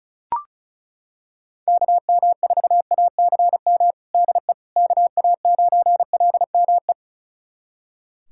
Morse code, heard as a keyed tone, about 36 wpm, 700 Hz. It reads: KM4ACM DE KA9LME
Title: USA Callsign DE Callsign 35wpm